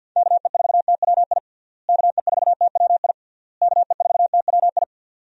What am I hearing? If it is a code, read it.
Morse code, heard as a keyed tone, about 52 wpm, 700 Hz. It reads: KE4TWI KE4TWI KE4TWI
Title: USA Call Signs - 6 Characters - Repeated 3x - 50wpm